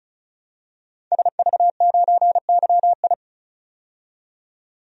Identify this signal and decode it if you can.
Morse code, heard as a keyed tone, about 36 wpm, 700 Hz. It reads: SV9YI
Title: DXCC-100 Call Signs - 5 Characters - 35wpm